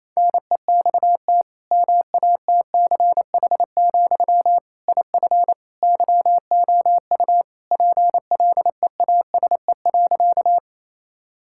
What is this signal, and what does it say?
Morse code, heard as a keyed tone, about 28 wpm, 700 Hz. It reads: NEXT MATCH, IF YOU PLEASE.
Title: Sentences from Top 1600 Words 28wpm